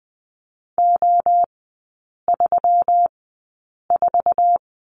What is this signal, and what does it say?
Morse code, heard as a keyed tone, about 20 wpm, spaced as for 8 wpm, 700 Hz. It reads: O34
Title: 152 Lesson 25e - Letter p - Three Characters 20wpm